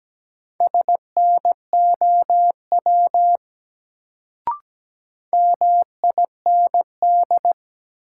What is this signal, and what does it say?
Morse code, heard as a keyed tone, about 17 wpm, 700 Hz. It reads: SNOW MIND
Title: Sentences from Top 500 Words - New Word Drill - 17wpm